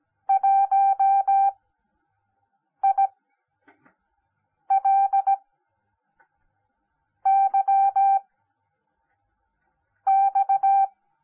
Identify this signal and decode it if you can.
Morse code, heard as a keyed tone, about 17 wpm, spaced as for 5 wpm, 780 Hz. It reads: 1ILYX
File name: Onlinelehrgang / morsen / DJ4UF Morsen10